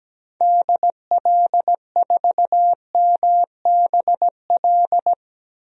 Morse code, heard as a keyed tone, about 17 wpm, 700 Hz. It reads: DL4MBL
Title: DXCC-100 Call Signs - 6 Characters - 17wpm